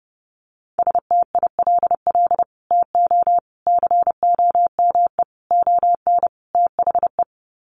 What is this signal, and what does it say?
Morse code, heard as a keyed tone, about 30 wpm, 700 Hz. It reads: STILL TO COME ON THE
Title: Sets of 5 Words Encore2 30wpm